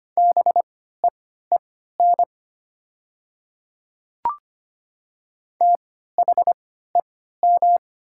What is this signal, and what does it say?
Morse code, heard as a keyed tone, about 25 wpm, spaced as for 14 wpm, 700 Hz. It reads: BEEN THEM
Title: 3x Character Spacing - Top 100 Words - 25wpm